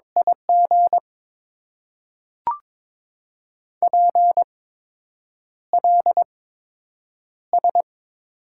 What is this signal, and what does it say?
Morse code, heard as a keyed tone, about 22 wpm, spaced as for 8 wpm, 690 Hz. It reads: IG PLS
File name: Top 90 QSO Elements - 8x Character Spacing - 22wpm